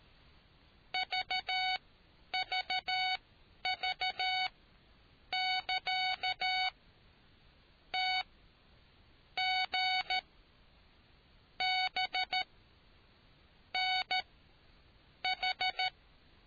Morse code, heard as a keyed tone, about 13 wpm, spaced as for 6 wpm, 750 Hz. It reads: VVV<KA>TGBNH